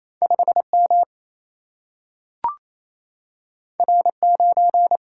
Morse code, heard as a keyed tone, about 28 wpm, 700 Hz. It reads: 5 M R 9
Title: Single-Letter Single-Number 28wpm